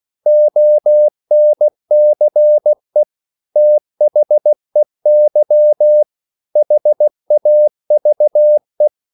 Morse code, heard as a keyed tone, about 16 wpm, 600 Hz. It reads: ONCE THEY HAVE
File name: CW / Practice / EnglishQuotations0416